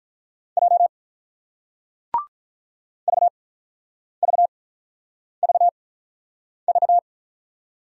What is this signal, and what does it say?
Morse code, heard as a keyed tone, about 44 wpm, spaced as for 9 wpm, 700 Hz. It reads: J VVVV
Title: Speed-Racing Single Letters 35wpm